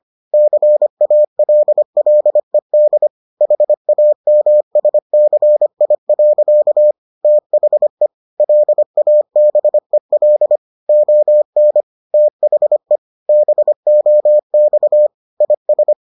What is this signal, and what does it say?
Morse code, heard as a keyed tone, about 25 wpm, 600 Hz. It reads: CALLED HAMSCI. THE LABEL ON THE BOX IS